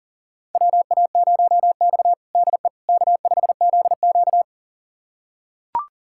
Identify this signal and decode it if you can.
Morse code, heard as a keyed tone, about 40 wpm, 700 Hz. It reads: WA0X DE K5ZQ
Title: USA Callsign DE Callsign - 4 Characters - 40wpm